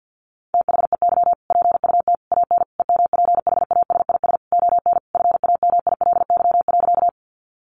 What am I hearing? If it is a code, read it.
Morse code, heard as a keyed tone, about 50 wpm, 700 Hz. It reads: THEY PUT AN EMPHASIS ON FAMILY.
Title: Sentences from Top 1400 Words 50wpm